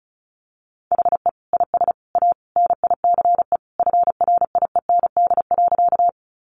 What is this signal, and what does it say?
Morse code, heard as a keyed tone, about 34 wpm, 700 Hz. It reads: HE IS A NICE FRIEND.
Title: Sentences from Top 1100 Words 35wpm